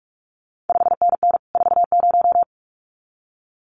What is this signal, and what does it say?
Morse code, heard as a keyed tone, about 46 wpm, 700 Hz. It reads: 5NN 40
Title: CQ WW DX Contest Exchange Encore - 45wpm